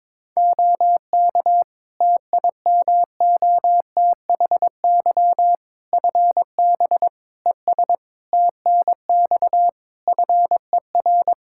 Morse code, heard as a keyed tone, about 22 wpm, 700 Hz. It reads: OK TIMOTHY FB ES TNX FER
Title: K1USN SST QSO Practice 22wpm